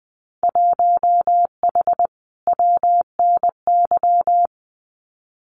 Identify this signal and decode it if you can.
Morse code, heard as a keyed tone, about 20 wpm, 700 Hz. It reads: 1H WNY
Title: Winter Field Day Exchanges Encore 20wpm